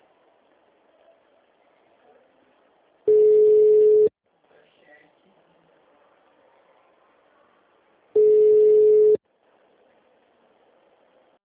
Ringback tone (Europe, 425 Hz).